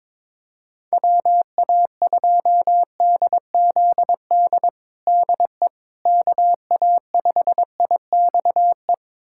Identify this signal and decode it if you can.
Morse code, heard as a keyed tone, about 22 wpm, 700 Hz. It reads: WA2DZD DE KA5IXE
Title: USA Callsign DE Callsign - 6 Characters - 22wpm